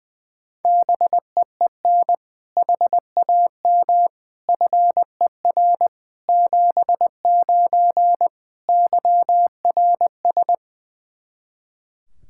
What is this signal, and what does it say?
Morse code, heard as a keyed tone, about 20 wpm, 700 Hz. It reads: BEEN HAM FER 79 YRS
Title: Practice QSO - Exchange 3 - More Info - 20wpm